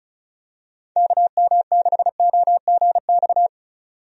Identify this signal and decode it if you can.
Morse code, heard as a keyed tone, about 34 wpm, 700 Hz. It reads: KM6OGX
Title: USA Call Signs - 6 Characters - 35wpm